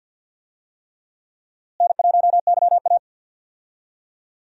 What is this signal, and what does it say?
Morse code, heard as a keyed tone, about 49 wpm, 700 Hz. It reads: N1YA
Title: USA Call Signs - 4 Characters - Repeated 3x - 50wpm